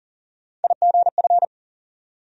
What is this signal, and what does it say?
Morse code, heard as a keyed tone, about 40 wpm, 700 Hz. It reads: IGF
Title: USA Call Sign Postfixes - 3 Letters 40wpm